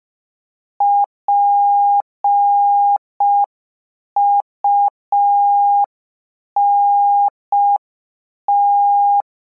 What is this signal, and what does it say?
Morse code, heard as a keyed tone, about 5 wpm, 810 Hz. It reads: PUNT